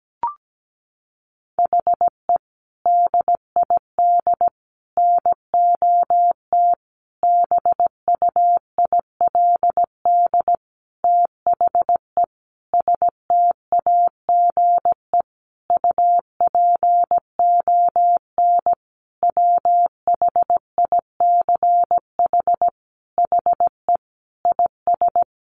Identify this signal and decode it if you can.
Morse code, heard as a keyed tone, about 17 wpm, 700 Hz. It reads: HE DID NOT BUILD THE STAGE UPON WHICH HE IS
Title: Sentences from Top 1200 Words 17wpm